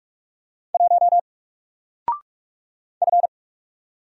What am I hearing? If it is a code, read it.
Morse code, heard as a keyed tone, about 44 wpm, 700 Hz. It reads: 1 F
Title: Single Letter-Number Alt 45wpm